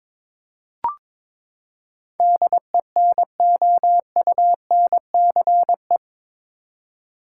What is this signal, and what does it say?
Morse code, heard as a keyed tone, about 22 wpm, 700 Hz. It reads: DENOUNCE
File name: practice / 22wpm / Eight Letter Words - 22wpm